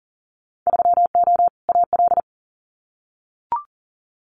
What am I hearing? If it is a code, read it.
Morse code, heard as a keyed tone, about 40 wpm, 700 Hz. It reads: 3O AL
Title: Winter Field Day Exchanges Encore 40wpm